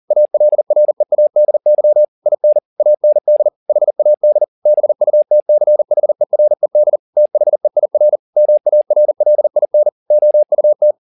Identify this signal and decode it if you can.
Morse code, heard as a keyed tone, about 40 wpm, 600 Hz. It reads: ALREADY IN AND HAD BUTCHERED THEIR MARLIN OUT